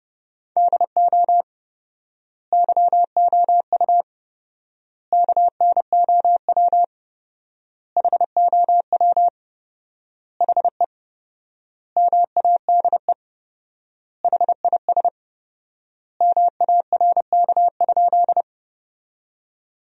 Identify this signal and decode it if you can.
Morse code, heard as a keyed tone, about 29 wpm, 700 Hz. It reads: DO YOU KNOW HOW HE MADE HIS MARK?
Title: 4x Word-Spacing - Sentences from Top 300 Words - 30wpm